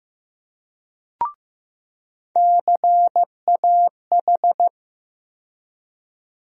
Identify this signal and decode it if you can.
Morse code, heard as a keyed tone, about 15 wpm, 700 Hz. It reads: CAH
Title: USA Call Sign Postfixes - 3 Letters 15wpm